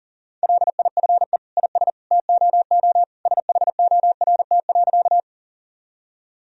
Morse code, heard as a keyed tone, about 40 wpm, 700 Hz. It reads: LIFE IS TOO SHORT.
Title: Sentences from Top 1000 Words - Review Part 2 - 40wpm